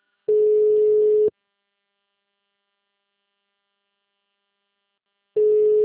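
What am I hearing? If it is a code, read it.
Ringback tone (Europe, 425 Hz).
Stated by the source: Outbound phone call recorded on a system in Arapongas - PR.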